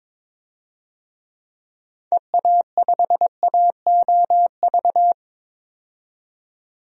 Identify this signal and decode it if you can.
Morse code, heard as a keyed tone, about 21 wpm, 700 Hz. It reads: EA5AOV
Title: DXCC-100 Call Signs - 6 Characters - 22wpm